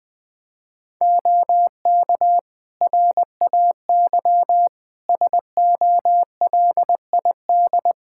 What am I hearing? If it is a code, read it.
Morse code, heard as a keyed tone, about 20 wpm, 700 Hz. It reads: OK RAY SOLID
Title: Practice QSO - Exchange 3 - More Info - 20wpm